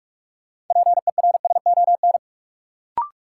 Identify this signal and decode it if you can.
Morse code, heard as a keyed tone, about 46 wpm, 700 Hz. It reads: PERSON
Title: Sentences from Top 400 Words - New Word Drill - 45wpm